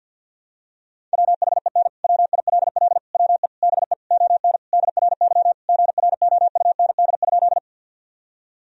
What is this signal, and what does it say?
Morse code, heard as a keyed tone, about 50 wpm, 700 Hz. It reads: WHEN WILL WE BE ON DRY GROUND?
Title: Sentences from Top 1000 Words - Review Part 3 - 50wpm